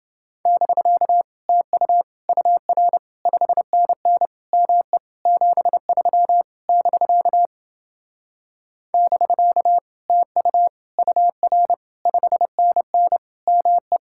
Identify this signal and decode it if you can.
Morse code, heard as a keyed tone, about 28 wpm, 700 Hz. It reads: <BK> TU UR 5NN ME 73 <BK> <BK> TU UR 5NN ME